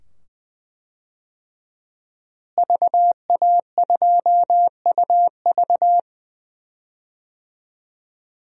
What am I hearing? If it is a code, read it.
Morse code, heard as a keyed tone, about 20 wpm, 700 Hz. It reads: VA2UV
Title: DXCC-100 Call Signs - 5 Characters - Repeated 2x - 20wpm